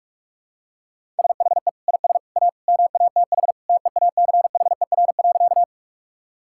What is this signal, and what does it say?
Morse code, heard as a keyed tone, about 45 wpm, 700 Hz. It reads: SHE IS A MATH TEACHER.
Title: Sentences from Top 1000 Words - Review Part 6 - 45wpm